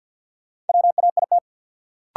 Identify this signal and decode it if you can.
Morse code, heard as a keyed tone, about 50 wpm, 700 Hz. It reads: WAIT
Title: Sentences from Top 500 Words - New Word Drill - 50wpm